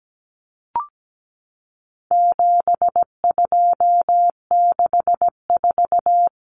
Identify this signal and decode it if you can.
Morse code, heard as a keyed tone, about 17 wpm, 700 Hz. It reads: 7264
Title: Numbers - 4-Digit 17wpm